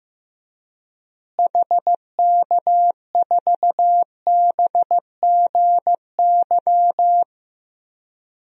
Morse code, heard as a keyed tone, about 15 wpm, 700 Hz. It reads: HK4BGY
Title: DXCC-100 Call Signs - 6 Characters - Repeated 3x - 15wpm